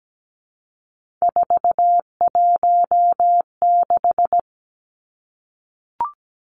Morse code, heard as a keyed tone, about 17 wpm, 700 Hz. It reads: 416
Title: Numbers - 3-Digit 17wpm